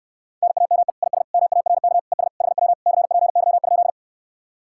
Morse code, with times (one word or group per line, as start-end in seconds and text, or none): 0.4-0.9 NAME
1.0-1.2 IS
1.3-2.0 MARK
2.1-2.3 ES
2.4-2.7 HW
2.9-3.9 CPY?